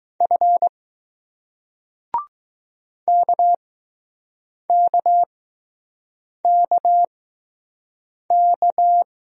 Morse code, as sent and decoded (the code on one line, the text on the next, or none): ..-. / -.- -.- -.- -.-
F KKKK